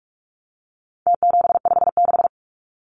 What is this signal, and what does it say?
Morse code, heard as a keyed tone, about 44 wpm, 700 Hz. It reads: T756
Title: Numbers - Contest Style - 4-Digit 45wpm